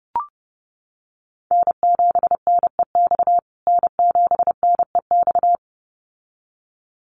Morse code, heard as a keyed tone, about 28 wpm, 700 Hz. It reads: N7NEX N7NEX
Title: USA Call Signs Repeated 2x 30wpm